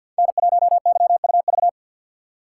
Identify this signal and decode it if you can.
Morse code, heard as a keyed tone, about 51 wpm, 700 Hz. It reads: N1YUV
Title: USA Call Signs - 3x Character Spacing - 50wpm